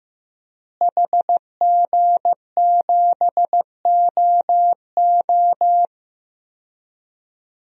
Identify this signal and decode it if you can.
Morse code, heard as a keyed tone, about 15 wpm, 700 Hz. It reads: HG7OO
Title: DXCC-100 Call Signs - 5 Characters - Repeated 3x - 15wpm